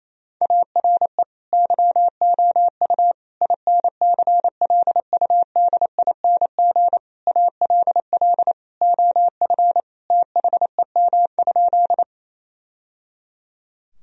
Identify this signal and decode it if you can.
Morse code, heard as a keyed tone, about 28 wpm, 700 Hz. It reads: ARE YOU INCLUDING ALL OF THEM?
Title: Sentences from Top 1000 Words - Review Part 4 - 28wpm